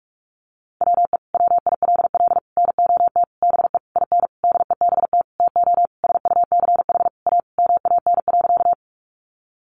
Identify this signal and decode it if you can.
Morse code, heard as a keyed tone, about 44 wpm, 700 Hz. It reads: WE WILL NOT BE IN DEBT TO SUCH A MAN.
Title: Sentences from Top 1300 Words 45wpm